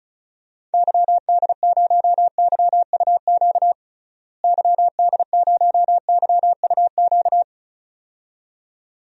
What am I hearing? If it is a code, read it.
Morse code, heard as a keyed tone, about 36 wpm, 700 Hz. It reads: YD0YUQ YD0YUQ
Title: DXCC-100 Call Signs - 6 Characters - Repeated 2x - 35wpm